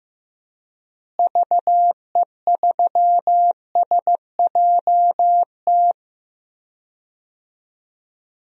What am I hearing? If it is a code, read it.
Morse code, heard as a keyed tone, about 15 wpm, 700 Hz. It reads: VE3SJT
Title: DXCC-100 Call Signs - 6 Characters - Repeated 3x - 15wpm